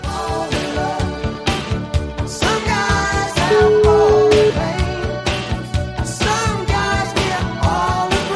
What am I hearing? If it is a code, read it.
Ringback tone (Europe, 425 Hz), with noise.